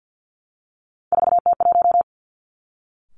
Morse code, heard as a keyed tone, about 50 wpm, 700 Hz. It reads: 4T1
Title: Three Challenging Letter-Number Combinations 50wpm